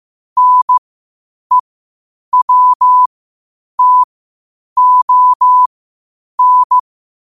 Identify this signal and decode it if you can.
Morse code, heard as a keyed tone, about 15 wpm, spaced as for 8 wpm, 1000 Hz. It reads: NEWTON
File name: qotdfiles / 8wpm